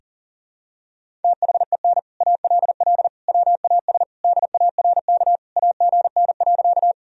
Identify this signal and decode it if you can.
Morse code, heard as a keyed tone, about 37 wpm, 700 Hz. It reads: THEN ALL WAS DARK AGN.
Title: Sentences from Top 500 Words and Abbreviations 40wpm